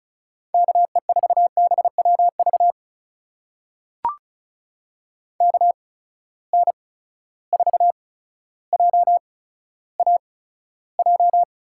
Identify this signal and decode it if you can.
Morse code, heard as a keyed tone, about 36 wpm, spaced as for 15 wpm, 700 Hz. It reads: KE4BWV KN4JAJ
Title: USA Call Signs - 6 Characters - 8x Character Spacing - 35wpm